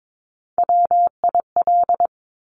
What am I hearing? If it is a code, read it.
Morse code, heard as a keyed tone, about 22 wpm, 700 Hz. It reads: WIL
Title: Top Three Letter Combinations in Words 22wpm